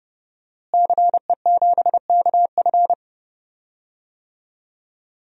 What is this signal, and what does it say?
Morse code, heard as a keyed tone, about 32 wpm, 700 Hz. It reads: CE7KF
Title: DXCC-100 Call Signs - 5 Characters - Repeated 3x - 30wpm